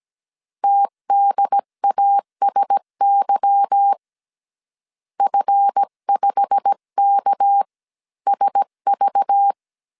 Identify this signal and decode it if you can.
Morse code, heard as a keyed tone, about 17 wpm, spaced as for 15 wpm, 800 Hz. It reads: TDASY F5KSV
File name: 1500C S B 16wpm